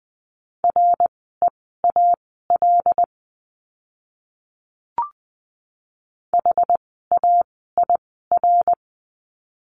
Morse code, heard as a keyed tone, about 20 wpm, spaced as for 14 wpm, 700 Hz. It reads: REAL HAIR
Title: 101 Lesson 17f - Letter u - Words 20wpm